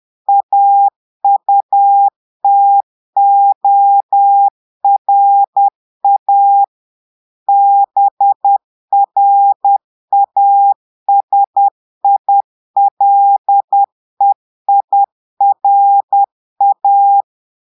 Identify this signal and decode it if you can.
Morse code, heard as a keyed tone, about 10 wpm, 800 Hz. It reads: AUTORA BRASILEIRA
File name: Portugais - texte en clair 1 - vitesse - 10 mots minute